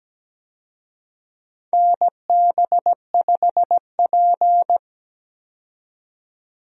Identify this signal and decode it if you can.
Morse code, heard as a keyed tone, about 17 wpm, 700 Hz. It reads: NB5P
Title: USA Call Signs - 4 Characters - 17wpm